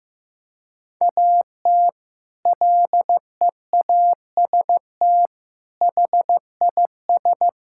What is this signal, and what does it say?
Morse code, heard as a keyed tone, about 15 wpm, 700 Hz. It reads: AT LEAST HIS
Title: Sentences from Top 1600 Words 15wpm